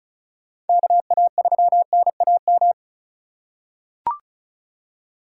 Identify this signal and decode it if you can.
Morse code, heard as a keyed tone, about 34 wpm, 700 Hz. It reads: KA3NAM
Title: K1USN SST QSO Practice - 35wpm